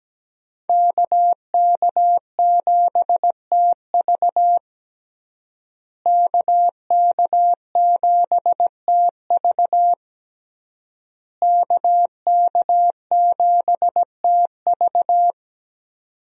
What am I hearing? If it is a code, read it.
Morse code, heard as a keyed tone, about 17 wpm, 700 Hz. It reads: KK7TV KK7TV KK7TV
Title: USA Call Signs - 5 Characters - Repeated 3x - 17wpm